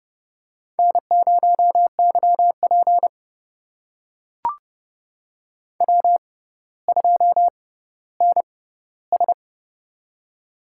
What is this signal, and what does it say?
Morse code, heard as a keyed tone, about 30 wpm, spaced as for 15 wpm, 700 Hz. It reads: N0YP W2NS
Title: USA Call Signs - 4 Characters - 6x Character Spacing - 30wpm